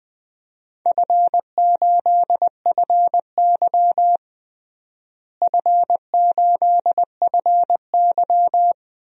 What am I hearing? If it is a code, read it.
Morse code, heard as a keyed tone, about 21 wpm, 700 Hz. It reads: F8FY F8FY
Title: DXCC-100 Call Signs - 4 Characters - Repeated 3x - 20wpm